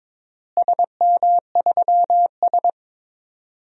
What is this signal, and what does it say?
Morse code, heard as a keyed tone, about 22 wpm, 700 Hz. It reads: SM3S
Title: DXCC-100 Call Signs - 4 Characters - Repeated 3x - 22wpm